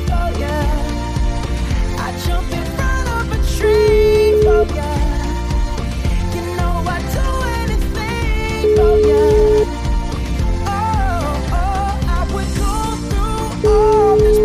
Ringback tone (Europe, 425 Hz), with noise.